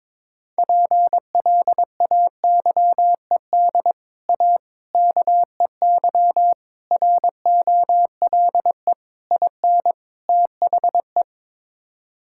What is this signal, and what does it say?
Morse code, heard as a keyed tone, about 22 wpm, 700 Hz. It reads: PLAYED A KEY ROLE IN THE
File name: practice / 22wpm / Sets of 6 Words - 22wpm